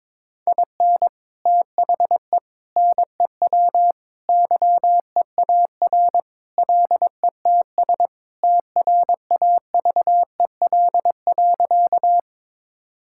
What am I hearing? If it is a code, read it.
Morse code, heard as a keyed tone, about 22 wpm, 700 Hz. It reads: IN THE NEW YEAR LETS TRAVEL.
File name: practice / 22wpm / Sentences with 3-Word Phrases - 22wpm